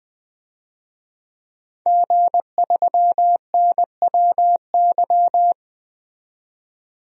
Morse code, heard as a keyed tone, about 20 wpm, 700 Hz. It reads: G3NWY
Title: DXCC-100 Call Signs - 5 Characters - 20wpm